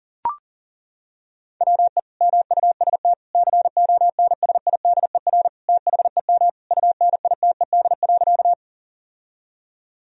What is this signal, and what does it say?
Morse code, heard as a keyed tone, about 40 wpm, 700 Hz. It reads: WE MUST CONSIDER THEM UNITED.
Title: Sentences from Top 1000 Words - Review Part 4 - 40wpm